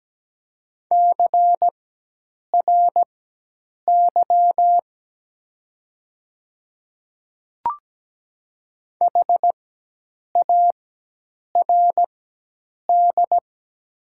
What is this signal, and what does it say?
Morse code, heard as a keyed tone, about 17 wpm, spaced as for 8 wpm, 700 Hz. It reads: CRY HARD
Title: 4x Character Spacing - Top 500 Words - 17wpm